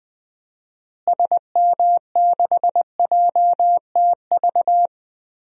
Morse code, heard as a keyed tone, about 20 wpm, 700 Hz. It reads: SM6JTV
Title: DXCC-100 Call Signs - 6 Characters - Repeated 2x - 20wpm